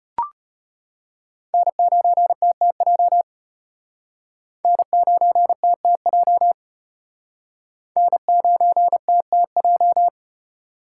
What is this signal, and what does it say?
Morse code, heard as a keyed tone, about 36 wpm, 700 Hz. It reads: N9TTJ N9TTJ N9TTJ